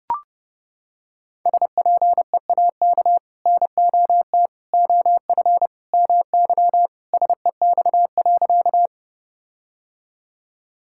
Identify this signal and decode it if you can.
Morse code, heard as a keyed tone, about 30 wpm, 700 Hz. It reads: SPEAK NOT OF MY SEX.
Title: Sentences from Top 1600 Words 30wpm